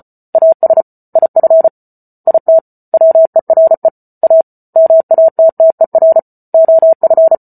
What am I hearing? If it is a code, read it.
Morse code, heard as a keyed tone, about 34 wpm, 650 Hz. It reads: AS IF IT WERE A MATTER OF